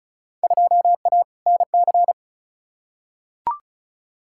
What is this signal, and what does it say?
Morse code, heard as a keyed tone, about 36 wpm, 700 Hz. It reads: 2A NC
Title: Field Day Exchanges - 35wpm